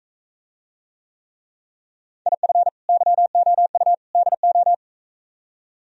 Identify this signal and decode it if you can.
Morse code, heard as a keyed tone, about 45 wpm, 700 Hz. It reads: IF YOU DO